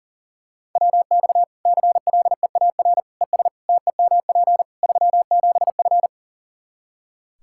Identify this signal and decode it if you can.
Morse code, heard as a keyed tone, about 40 wpm, 700 Hz. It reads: WX CLEAR ES TEMP 37F
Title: K1USN SST QSO Practice 40wpm